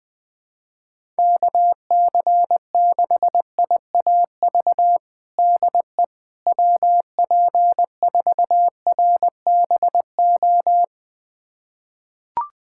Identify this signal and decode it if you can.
Morse code, heard as a keyed tone, about 20 wpm, 700 Hz. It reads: KC6IAV DE WP4RBO
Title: USA Callsign DE Callsign 20wpm